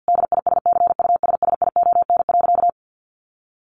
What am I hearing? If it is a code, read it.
Morse code, heard as a keyed tone, about 50 wpm, 700 Hz. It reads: DISCUSSION.
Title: Sentences from Top 1500 Words 50wpm